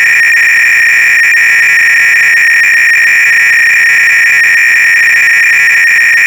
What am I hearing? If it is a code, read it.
300-baud modem data.